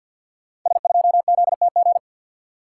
Morse code, heard as a keyed tone, about 52 wpm, 690 Hz. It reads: S2ZTG
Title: Code Groups - 5 Letter:Numbers - 50wpm